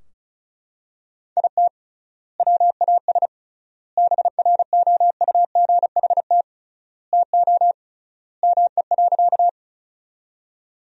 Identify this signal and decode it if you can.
Morse code, heard as a keyed tone, about 36 wpm, 700 Hz. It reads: IT WAS BROUGHT TO ME.
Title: 3x Word-Spacing - Sentences from Top 500 Words - 35wpm